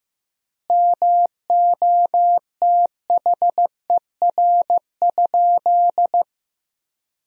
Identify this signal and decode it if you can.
Morse code, heard as a keyed tone, about 15 wpm, 700 Hz. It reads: MOTHER?
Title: 3x Word-Spacing - Sentences from Top 200 Words - 15wpm